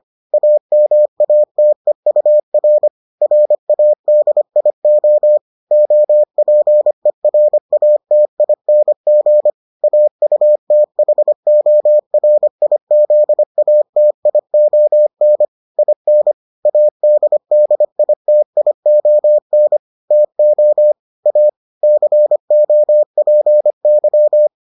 Morse code, heard as a keyed tone, about 25 wpm, 600 Hz. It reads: AMATEUR RADIO OPERATING AUTHORIZATION IN ADDITION TO A COPY